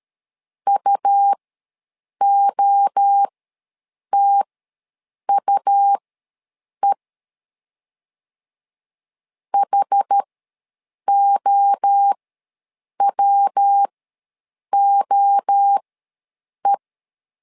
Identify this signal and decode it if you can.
Morse code, heard as a keyed tone, about 13 wpm, spaced as for 7 wpm, 800 Hz. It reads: UOTUE HOWOE
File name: LEZIONI / 3 MWRO / 3 MWRO ws 12wpm 1914